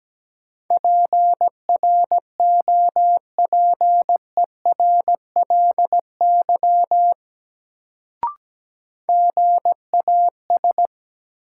Morse code, heard as a keyed tone, about 17 wpm, 700 Hz. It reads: PROPERLY GAS
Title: Sentences from Top 1100 Words - New Word Drill - 17wpm